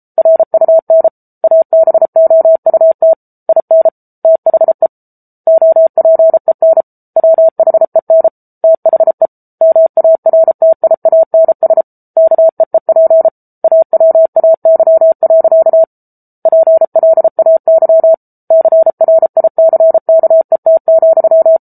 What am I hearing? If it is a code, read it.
Morse code, heard as a keyed tone, about 34 wpm, 650 Hz. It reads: RUN ABOUT IN THE OPEN WHEN THE MARTIANS KEEP AWAY. PLAY CRICKET,
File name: Book Two / 07-Chapter 7 - 33 WPM